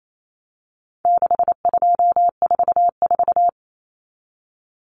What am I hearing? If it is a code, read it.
Morse code, heard as a keyed tone, about 28 wpm, 700 Hz. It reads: 6244
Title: Numbers - 4-Digit 28wpm